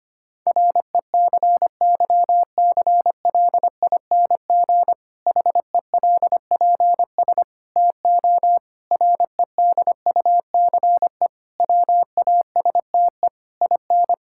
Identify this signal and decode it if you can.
Morse code, heard as a keyed tone, about 25 wpm, 700 Hz. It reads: RECYCLING HELPS TO REDUCE WASTE IN
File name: practice / 25wpm / Difficult Sentences with Hints - 25wpm